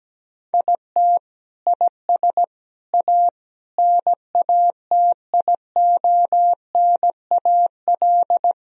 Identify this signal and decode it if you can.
Morse code, heard as a keyed tone, about 17 wpm, 700 Hz. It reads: IT IS A NATIONAL
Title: Sentences from Top 1000 Words - Review Part 4 - 17wpm